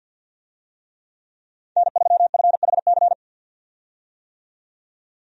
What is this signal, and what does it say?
Morse code, heard as a keyed tone, about 53 wpm, 700 Hz. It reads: N3FHC
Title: USA Call Signs - 5 Characters - 50wpm